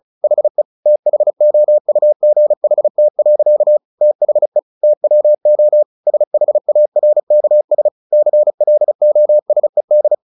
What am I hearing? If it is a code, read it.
Morse code, heard as a keyed tone, about 34 wpm, 600 Hz. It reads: HE THOUGHT. THE TWO SHARKS CLOSED
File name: the old man and the sea 2 35wpm